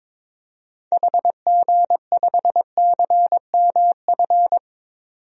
Morse code, heard as a keyed tone, about 22 wpm, 700 Hz. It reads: HG5CMF
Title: DXCC-100 Call Signs - 6 Characters - 22wpm